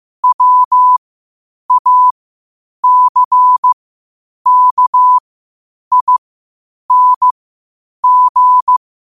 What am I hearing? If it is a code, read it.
Morse code, heard as a keyed tone, about 15 wpm, spaced as for 8 wpm, 1000 Hz. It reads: WACKING
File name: qotdfiles / 8wpm